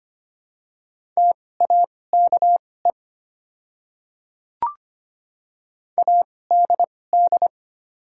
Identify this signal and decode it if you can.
Morse code, heard as a keyed tone, about 25 wpm, spaced as for 18 wpm, 700 Hz. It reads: TAKE ADD
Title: 2x Character Spacing - Top 200 Words - 25wpm